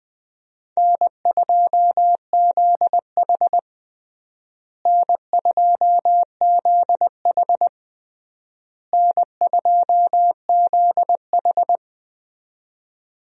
Morse code, heard as a keyed tone, about 20 wpm, 700 Hz. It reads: N2ZH N2ZH N2ZH
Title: USA Call Signs - 4 Characters - Repeated 3x - 20wpm